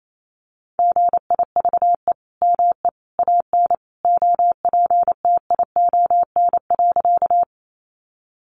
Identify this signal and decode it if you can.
Morse code, heard as a keyed tone, about 28 wpm, 700 Hz. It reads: GIVE ME AN OPTION.
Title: Sentences from Top 1300 Words 28wpm